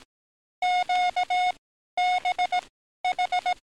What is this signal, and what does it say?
Morse code, heard as a keyed tone, about 18 wpm, spaced as for 12 wpm, 720 Hz. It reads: QBH
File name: training / NAVPERS 11049to11060 / NAVPERS 11052 RA RB